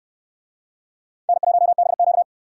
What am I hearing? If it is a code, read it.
Morse code, heard as a keyed tone, about 69 wpm, 700 Hz. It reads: N1BQ